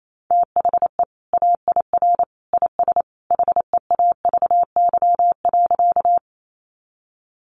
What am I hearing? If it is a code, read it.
Morse code, heard as a keyed tone, about 28 wpm, 700 Hz. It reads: THE AIR IS HEAVY.
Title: Sentences from Top 1100 Words 28wpm